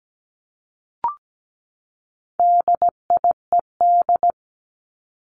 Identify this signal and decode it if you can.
Morse code, heard as a keyed tone, about 17 wpm, 700 Hz. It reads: DIED
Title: ED Suffix - Top 100 Words with Base Words 17wpm